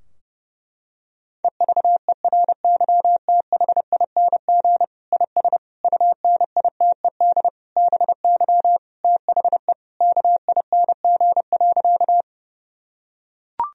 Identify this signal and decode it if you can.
Morse code, heard as a keyed tone, about 32 wpm, 700 Hz. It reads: EVERYTHING IS UNITED BY THE KING.
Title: Sentences from Top 1000 Words - Review Part 4 - 30wpm